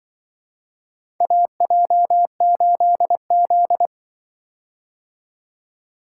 Morse code, heard as a keyed tone, about 24 wpm, 700 Hz. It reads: AJ8Z
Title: Speed-Racing - Four Letter-Number Combinations - 20wpm